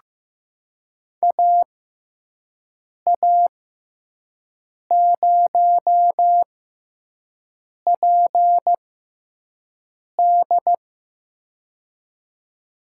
Morse code, heard as a keyed tone, about 15 wpm, spaced as for 5 wpm, 700 Hz. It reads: AA0PD
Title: USA Call Signs - 6x Character Spacing - 15wpm